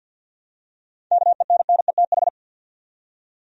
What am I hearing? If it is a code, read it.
Morse code, heard as a keyed tone, about 51 wpm, 700 Hz. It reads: KENNETH
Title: USA Common Names - 50wpm